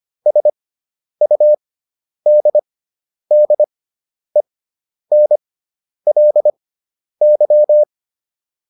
Morse code, heard as a keyed tone, about 25 wpm, spaced as for 10 wpm, 600 Hz. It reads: SUDDENLY